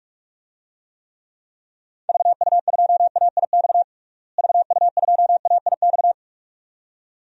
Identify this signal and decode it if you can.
Morse code, heard as a keyed tone, about 43 wpm, 700 Hz. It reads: VU2AIX VU2AIX
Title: DXCC-100 Call Signs - 6 Characters - Repeated 2x - 45wpm